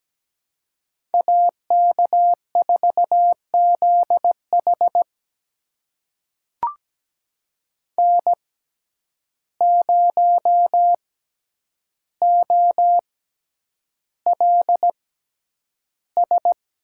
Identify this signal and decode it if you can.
Morse code, heard as a keyed tone, about 17 wpm, spaced as for 9 wpm, 700 Hz. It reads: AK4ZH N0OLS
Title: USA Call Signs - 5 Characters - 6x Character Spacing - 17wpm